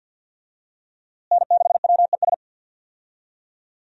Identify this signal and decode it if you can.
Morse code, heard as a keyed tone, about 45 wpm, 700 Hz. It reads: N6WES
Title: USA Call Signs - 5 Characters - Repeated 3x - 50wpm